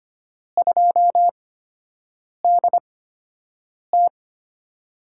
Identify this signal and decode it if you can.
Morse code, heard as a keyed tone, about 25 wpm, spaced as for 7 wpm, 700 Hz. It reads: 2DT
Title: USA Call Signs - 8x Character Spacing - 25wpm